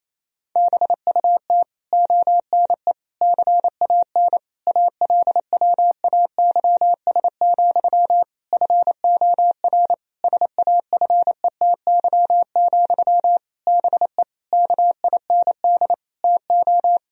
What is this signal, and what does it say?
Morse code, heard as a keyed tone, about 28 wpm, 700 Hz. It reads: BUT ONE CAN ALWAYS, FOR SAFETY, BE KIND TO
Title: Sentences from Top 1000 Words - Review Part 4 - 28wpm